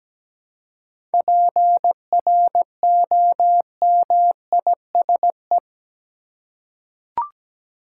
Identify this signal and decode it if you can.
Morse code, heard as a keyed tone, about 17 wpm, 700 Hz. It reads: PROMISE
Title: Seven Letter Words - 17wpm